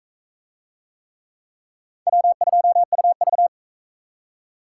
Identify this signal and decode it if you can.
Morse code, heard as a keyed tone, about 38 wpm, 700 Hz. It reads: W2UV